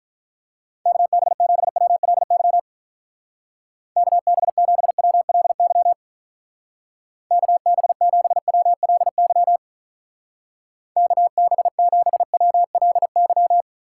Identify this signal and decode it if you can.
Morse code, heard as a keyed tone, about 43 wpm, 700 Hz. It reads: KB7WLY KB7WLY KB7WLY KB7WLY
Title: Speed Racing - USA 2x3 Call Signs 35wpm